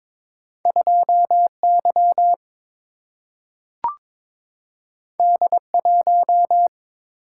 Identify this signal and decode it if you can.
Morse code, heard as a keyed tone, about 22 wpm, 700 Hz. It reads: 2Y D1
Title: Single-Letter Single-Number 22wpm